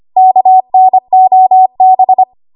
Morse code, heard as a keyed tone, about 25 wpm, 750 Hz. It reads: KNOB
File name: Archive / 25 WPM / 25WPM